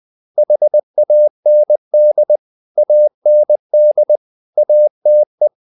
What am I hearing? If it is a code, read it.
Morse code, heard as a keyed tone, about 20 wpm, 600 Hz. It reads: HAND AND ATE
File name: the old man and the sea part 2 20wpm